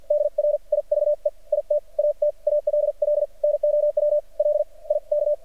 Morse code, heard as a keyed tone, about 50 wpm, spaced as for 39 wpm, 590 Hz. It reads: DRIVE IT AT ALL NOW LIK
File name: archived / CRSnet